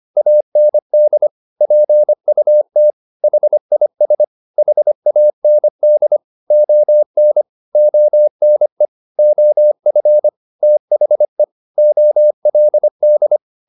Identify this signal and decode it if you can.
Morse code, heard as a keyed tone, about 25 wpm, 600 Hz. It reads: AND PUT HIS HAND ON ONE OF THE OLD
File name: the old man and the sea part 1 25wpm